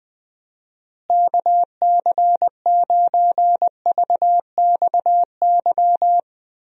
Morse code, heard as a keyed tone, about 20 wpm, 700 Hz. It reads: KC9VXY
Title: USA Call Signs - 6 Characters - Repeated 2x - 20wpm